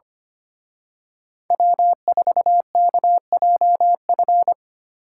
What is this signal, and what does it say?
Morse code, heard as a keyed tone, about 28 wpm, 700 Hz. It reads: W4KJF
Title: Speed Racing - USA Call Signs 25wpm